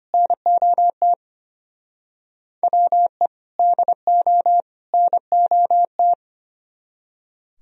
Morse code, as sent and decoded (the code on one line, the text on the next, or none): -. --- - / .-- . / -.. --- / -. --- -
NOT WE DO NOT